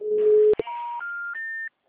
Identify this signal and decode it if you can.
Special information tone.
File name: out- -2007- - - .72972